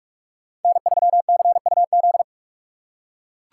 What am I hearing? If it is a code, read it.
Morse code, heard as a keyed tone, about 41 wpm, 700 Hz. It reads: N3CUZ